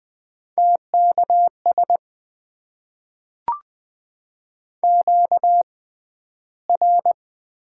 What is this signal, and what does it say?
Morse code, heard as a keyed tone, about 20 wpm, spaced as for 12 wpm, 700 Hz. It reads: TKS QR
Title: Top 90 QSO Elements - 6x Character Spacing - 20wpm